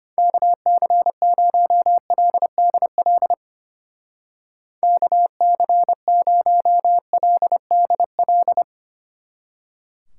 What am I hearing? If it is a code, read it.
Morse code, heard as a keyed tone, about 31 wpm, 700 Hz. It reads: KC0LDL KC0LDL
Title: Speed Racing - USA Call Signs 25wpm